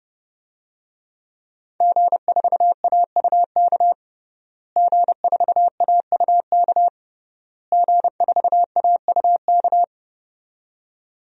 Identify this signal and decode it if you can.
Morse code, heard as a keyed tone, about 30 wpm, 700 Hz. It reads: G4AUK G4AUK G4AUK
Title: DXCC-100 Call Signs - 5 Characters - Repeated 3x - 30wpm